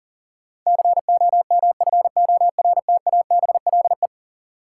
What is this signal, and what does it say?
Morse code, heard as a keyed tone, about 40 wpm, 700 Hz. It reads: COMFORTABLE
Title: Sentences from Top 1000 Words - Review Part 7 - New Word Drill - 40wpm